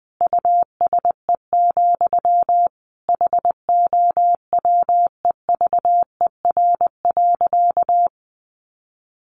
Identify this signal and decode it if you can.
Morse code, heard as a keyed tone, about 20 wpm, 700 Hz. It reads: USE, HOWEVER.
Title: Sentences from Top 1400 Words 20wpm